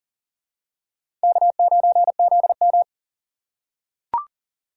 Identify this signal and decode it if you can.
Morse code, heard as a keyed tone, about 37 wpm, 700 Hz. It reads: K9ZM
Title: USA Call Signs - 4 Characters - 3x Character Spacing - 40wpm